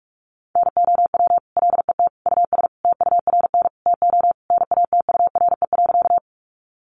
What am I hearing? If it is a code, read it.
Morse code, heard as a keyed tone, about 44 wpm, 700 Hz. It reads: NOW LET US TURN TO NATURE.
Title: Sentences from Top 600 Words 45wpm